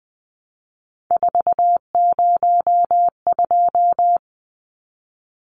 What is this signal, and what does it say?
Morse code, heard as a keyed tone, about 20 wpm, 700 Hz. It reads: 402
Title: Numbers - 3-Digit 20wpm